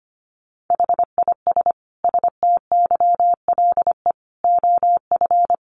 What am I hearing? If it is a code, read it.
Morse code, heard as a keyed tone, about 25 wpm, 700 Hz. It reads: HIS STYLE OF
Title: Sentences from Top 800 Words 25wpm